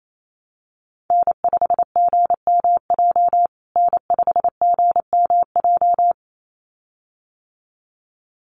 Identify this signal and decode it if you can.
Morse code, heard as a keyed tone, about 28 wpm, 700 Hz. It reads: N5GMJ N5GMJ
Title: USA Call Signs Repeated 2x 28wpm